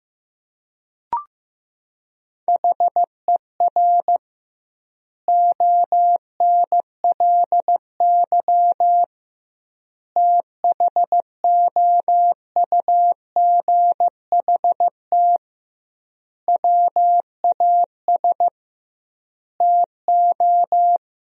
Morse code, heard as a keyed tone, about 15 wpm, 700 Hz. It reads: HER ONLY THOUGHT WAS TO
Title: 2x Word-Spacing - Sentences from Top 400 Words - 15wpm